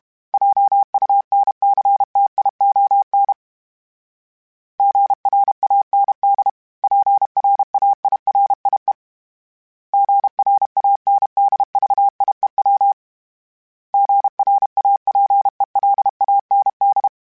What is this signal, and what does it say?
Morse code, heard as a keyed tone, about 32 wpm, 800 Hz. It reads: JUNCTION GRAND PRAIRIE GRANDVIEW GRAPELAND
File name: fromSMs / FromSMs mp3s / Cities@30